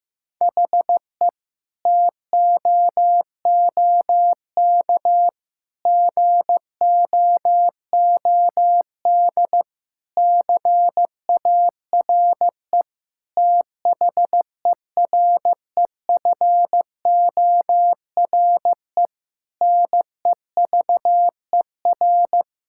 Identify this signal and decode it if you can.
Morse code, heard as a keyed tone, about 15 wpm, 700 Hz. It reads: HE TOOK GOOD CARE THEREFORE NEVER
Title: Sentences from Top 1000 Words - Review Part 5 - 15wpm